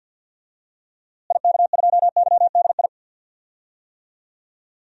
Morse code, heard as a keyed tone, about 46 wpm, 700 Hz. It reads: IK2YDI
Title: DXCC-100 Call Signs - 6 Characters - Repeated 3x - 50wpm